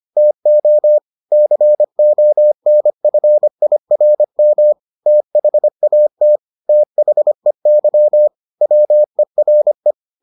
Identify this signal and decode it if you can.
Morse code, heard as a keyed tone, about 25 wpm, 600 Hz. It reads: TO CONFIRM THAT THEY WERE